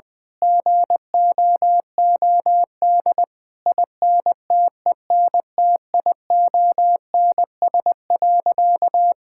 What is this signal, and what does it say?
Morse code, heard as a keyed tone, about 21 wpm, 700 Hz. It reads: GOOD INTENTIONS.
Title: Sentences with 8-Word Phrases - 20wpm